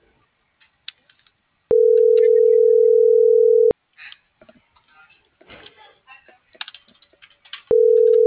Ringback tone (North America), 440 + 480 Hz.
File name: CENTRAL FRALSE / out brendaaviles.j